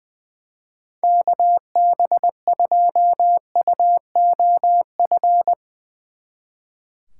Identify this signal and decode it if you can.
Morse code, heard as a keyed tone, about 21 wpm, 700 Hz. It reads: KB2UOF
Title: Speed Racing - USA 2x3 Call Signs 20wpm